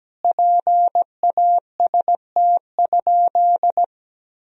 Morse code, heard as a keyed tone, about 17 wpm, 700 Hz. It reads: PAST?
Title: Sentences from Top 1000 Words - Review Part 4 - 17wpm